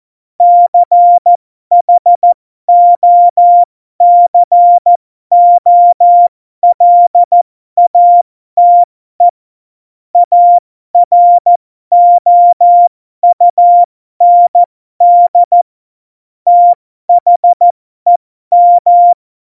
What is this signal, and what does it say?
Morse code, heard as a keyed tone, about 14 wpm, spaced as for 12 wpm, 700 Hz. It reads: CHOCOLATE AROUND THEM